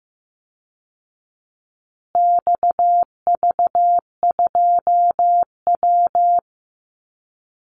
Morse code, heard as a keyed tone, about 15 wpm, 700 Hz. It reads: XV2W
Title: Call Signs Repeated 3x - 15wpm